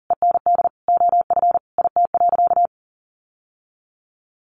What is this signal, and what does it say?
Morse code, heard as a keyed tone, about 40 wpm, 700 Hz. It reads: END OF IT.
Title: Sentences from Top 1200 Words 40wpm